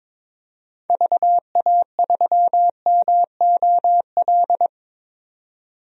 Morse code, heard as a keyed tone, about 21 wpm, 700 Hz. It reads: VA3MOL
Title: DXCC-100 Call Signs - 6 Characters - 22wpm